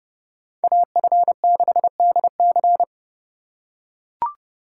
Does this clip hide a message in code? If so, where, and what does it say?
Yes: Morse code from 0.6 to 2.8 s, heard as a keyed tone, about 29 wpm, 700 Hz. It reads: AF6DC